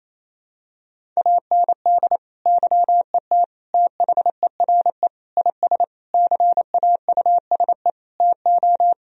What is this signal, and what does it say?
Morse code, heard as a keyed tone, about 28 wpm, 700 Hz. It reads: AND YET THERE IS CAUSE TO
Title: Sentences from Top 1000 Words - Review Part 1 - 28wpm